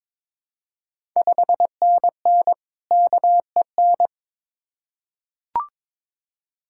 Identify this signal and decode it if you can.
Morse code, heard as a keyed tone, about 23 wpm, 690 Hz. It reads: 5NN KEN
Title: Ontario QSO Party Exchange 22wpm